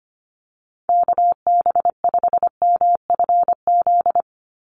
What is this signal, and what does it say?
Morse code, heard as a keyed tone, about 25 wpm, 700 Hz. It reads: KB5MFZ
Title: USA Call Signs 25wpm